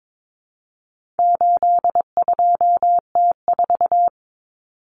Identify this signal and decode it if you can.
Morse code, heard as a keyed tone, about 22 wpm, 700 Hz. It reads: 82T4
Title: Common Cut Numbers - 4-Digit 22wpm